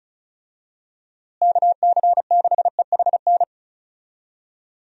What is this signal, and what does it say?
Morse code, heard as a keyed tone, about 34 wpm, 700 Hz. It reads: KC6EHN
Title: USA Call Signs - 6 Characters - Repeated 3x - 35wpm